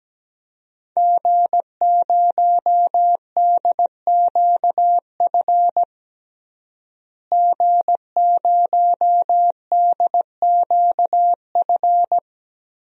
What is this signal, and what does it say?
Morse code, heard as a keyed tone, about 17 wpm, 700 Hz. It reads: G0DQF G0DQF
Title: DXCC-100 Call Signs - 5 Characters - Repeated 3x - 17wpm